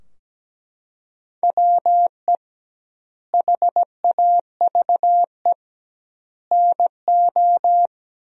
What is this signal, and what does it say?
Morse code, heard as a keyed tone, about 17 wpm, 700 Hz. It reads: WE HAVE NO
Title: 2x Word-Spacing - Sentences from Top 100 Words - 17wpm